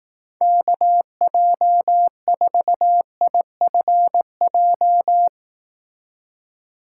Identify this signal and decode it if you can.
Morse code, heard as a keyed tone, about 18 wpm, 700 Hz. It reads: KJ4IFJ
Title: Speed Racing - USA 2x3 Call Signs 15wpm